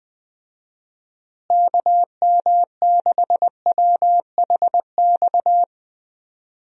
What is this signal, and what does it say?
Morse code, heard as a keyed tone, about 20 wpm, 700 Hz. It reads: KM6WHX
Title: USA Call Signs - 6 Characters - Repeated 3x - 20wpm